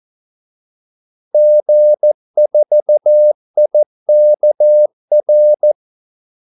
Morse code, heard as a keyed tone, about 14 wpm, 600 Hz. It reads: G4IKR
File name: Callsigns-14wpm